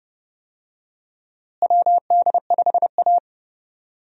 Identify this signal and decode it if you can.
Morse code, heard as a keyed tone, about 30 wpm, 700 Hz. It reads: WD5A
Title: USA Call Signs - 4 Characters - Repeated 3x - 30wpm